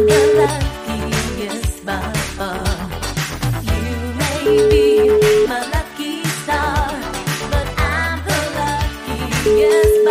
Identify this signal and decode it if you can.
Ringback tone (Europe, 425 Hz), with noise.